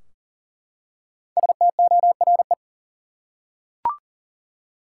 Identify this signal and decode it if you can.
Morse code, heard as a keyed tone, about 40 wpm, 700 Hz. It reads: STORE
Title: Five Letter Words - 40wpm